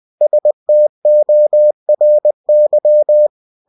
Morse code, heard as a keyed tone, about 20 wpm, 600 Hz. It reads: STORY
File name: cw-books / winnie-the-pooh / winnie-20wpm-page-0101